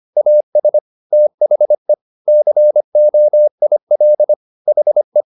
Morse code, heard as a keyed tone, about 25 wpm, 600 Hz. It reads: AS THE COIL HE
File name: the old man and the sea part 1 25wpm